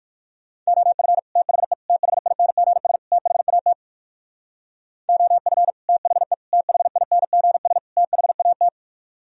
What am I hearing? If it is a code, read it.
Morse code, heard as a keyed tone, about 66 wpm, spaced as for 57 wpm, 700 Hz. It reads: OF THE THINGS THAT OF THE THINGS THAT